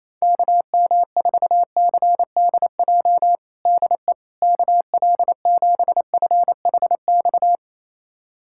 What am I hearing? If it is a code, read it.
Morse code, heard as a keyed tone, about 28 wpm, 700 Hz. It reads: KM4CDJ DE KL7FHX
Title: USA Callsign DE Callsign - 6 Characters - 28wpm